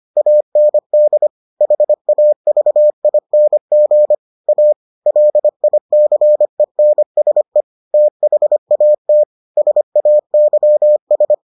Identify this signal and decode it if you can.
Morse code, heard as a keyed tone, about 25 wpm, 600 Hz. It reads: AND HAVING A LICENSE THAT SAYS